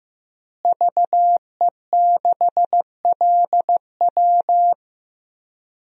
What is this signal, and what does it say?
Morse code, heard as a keyed tone, about 15 wpm, 700 Hz. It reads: VE6LW
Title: DXCC-100 Call Signs - 5 Characters - Repeated 3x - 15wpm